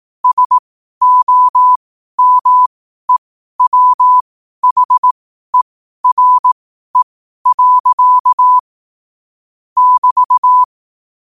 Morse code, heard as a keyed tone, about 18 wpm, spaced as for 12 wpm, 1000 Hz. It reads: SOMEWHERE. <BT>
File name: qotdfiles / 12wpm